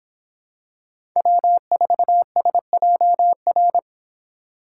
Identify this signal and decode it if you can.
Morse code, heard as a keyed tone, about 26 wpm, 700 Hz. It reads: W4SJR